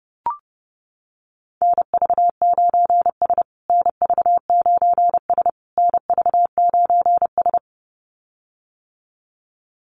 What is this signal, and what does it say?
Morse code, heard as a keyed tone, about 30 wpm, 700 Hz. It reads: NV9S NV9S NV9S
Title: USA Call Signs Repeated 3x 30wpm